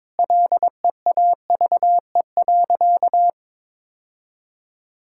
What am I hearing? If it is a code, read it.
Morse code, heard as a keyed tone, about 22 wpm, 700 Hz. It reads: LEAVE.
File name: practice / 22wpm / Sentences with 4-Word Phrases - 22wpm